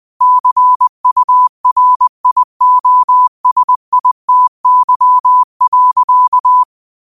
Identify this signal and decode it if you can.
Morse code, heard as a keyed tone, about 20 wpm, 1000 Hz. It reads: CURIOSITY.